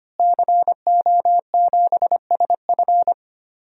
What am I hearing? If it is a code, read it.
Morse code, heard as a keyed tone, about 25 wpm, 700 Hz. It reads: CO7SF
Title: DXCC-100 Call Signs - 5 Characters - Repeated 2x - 25wpm